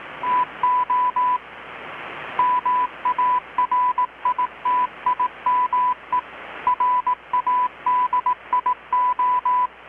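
Morse code, heard as a keyed tone, about 18 wpm, 1000 Hz. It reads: TO MARITIME RADIO
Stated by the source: Morse from KPH's Night of Nights XX maritime radiotelegraphy event, received in Chiapas, Mexico about 3,800 km away on a Kenwood R-600 with a grounded 10 m long-wire antenna.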